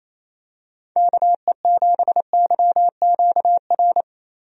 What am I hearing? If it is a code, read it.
Morse code, heard as a keyed tone, about 28 wpm, 700 Hz. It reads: KE7YQR
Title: USA Call Signs - 6 Characters - Repeated 2x - 28wpm